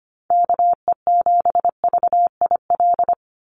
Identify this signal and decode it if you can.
Morse code, heard as a keyed tone, about 25 wpm, 700 Hz. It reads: KE7VIL
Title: USA Call Signs 25wpm